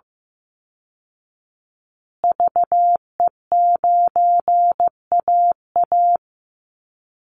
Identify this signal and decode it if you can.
Morse code, heard as a keyed tone, about 15 wpm, 700 Hz. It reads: VE9AA
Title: Call Signs Repeated 3x - 15wpm